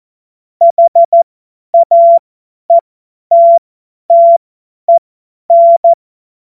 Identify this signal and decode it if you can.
Morse code, heard as a keyed tone, about 14 wpm, spaced as for 10 wpm, 680 Hz. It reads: HAETTEN
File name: Morsekurs / Witze / Witz 14.November 2023 50 F70